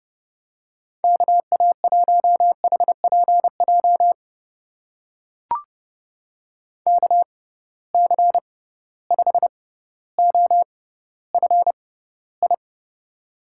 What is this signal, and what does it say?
Morse code, heard as a keyed tone, about 30 wpm, spaced as for 15 wpm, 700 Hz. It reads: KA1HPJ KC5OFI
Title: USA Call Signs - 6 Characters - 6x Character Spacing - 30wpm